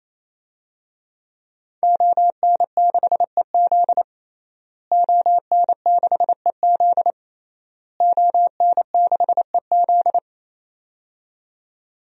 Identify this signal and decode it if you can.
Morse code, heard as a keyed tone, about 28 wpm, 700 Hz. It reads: ON6EZ ON6EZ ON6EZ
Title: DXCC-100 Call Signs - 5 Characters - Repeated 3x - 28wpm